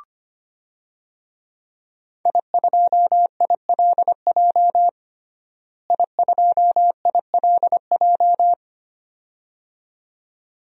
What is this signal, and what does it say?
Morse code, heard as a keyed tone, about 25 wpm, 700 Hz. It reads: I2ILJ I2ILJ
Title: DXCC-100 Call Signs - 5 Characters - Repeated 2x - 25wpm